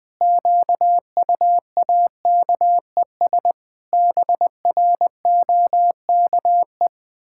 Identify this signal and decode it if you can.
Morse code, heard as a keyed tone, about 20 wpm, 700 Hz. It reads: QUAKES BROKE
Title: Pangram Challenge v2 - 20wpm